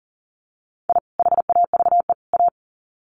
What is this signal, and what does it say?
Morse code, heard as a keyed tone, about 40 wpm, 700 Hz. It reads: I HAVE A
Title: Sets of 3 Words 40wpm